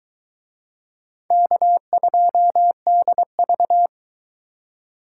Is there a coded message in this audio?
Yes: Morse code from 1.3 to 3.9 s, heard as a keyed tone, about 22 wpm, 700 Hz. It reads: K2DV